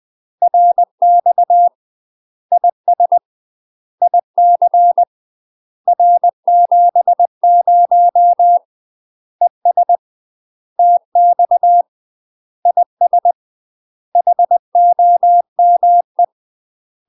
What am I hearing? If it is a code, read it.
Morse code, heard as a keyed tone, about 20 wpm, 700 Hz. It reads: RX IS IC R70 ES TX IS HOME
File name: UK MASTER QSOs TRAINING / 20WPM / UK MASTER QSOs 0044